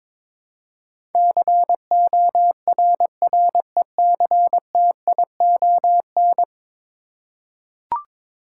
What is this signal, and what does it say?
Morse code, heard as a keyed tone, about 22 wpm, 700 Hz. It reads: CORRECTION
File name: practice / 22wpm / 10 Letter Words - First 3 Letters Given - 22wpm